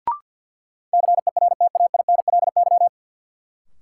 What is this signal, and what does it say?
Morse code, heard as a keyed tone, about 48 wpm, 700 Hz. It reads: CERTAINLY
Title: Sentences from Top 900 Words - New Word Drill - 50wpm